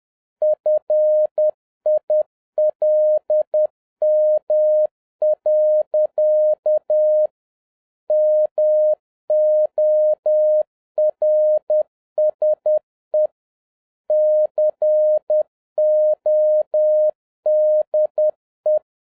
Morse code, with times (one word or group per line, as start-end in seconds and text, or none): 0.4-7.3 FILM.
8.1-13.3 MORSE
14.1-18.8 CODE